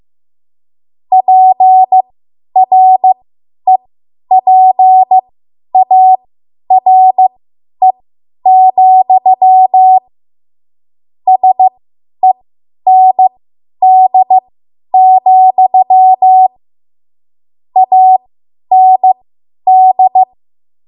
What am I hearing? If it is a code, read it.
Morse code, heard as a keyed tone, about 15 wpm, spaced as for 10 wpm, 750 Hz. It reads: PREPARE, SEND, AND